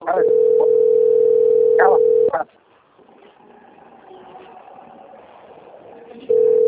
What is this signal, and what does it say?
Ringback tone (North America), 440 + 480 Hz, with speech.
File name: rg-1600-unknown- - - .1242